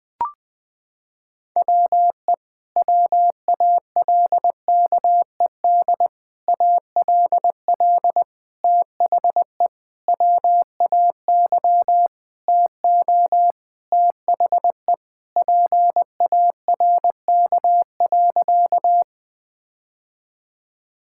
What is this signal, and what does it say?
Morse code, heard as a keyed tone, about 20 wpm, 700 Hz. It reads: WE WALKED ALL THE WAY TO THE PARK.
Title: Sentences with 4-Word Phrases - 20wpm